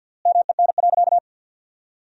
Morse code, heard as a keyed tone, about 52 wpm, 700 Hz. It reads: MEN.
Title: 3x Word-Spacing - Sentences from Top 300 Words - 50wpm